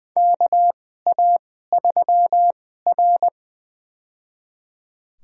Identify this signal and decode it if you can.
Morse code, heard as a keyed tone, about 20 wpm, spaced as for 15 wpm, 700 Hz. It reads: KA3R
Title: USA Call Signs - 2x Character Spacing - 20wpm